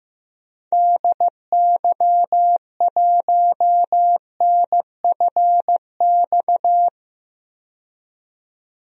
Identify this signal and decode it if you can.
Morse code, heard as a keyed tone, about 15 wpm, 700 Hz. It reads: DY1NFX
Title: DXCC-100 Call Signs - 6 Characters - 15wpm